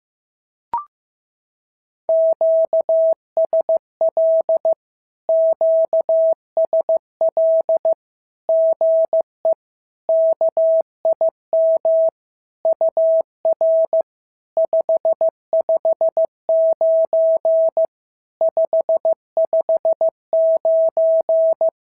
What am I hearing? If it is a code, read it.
Morse code, heard as a keyed tone, about 15 wpm, 650 Hz. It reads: QSL QSL GE KIM UR 559 559
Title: SOTA QSO Practice 15wpm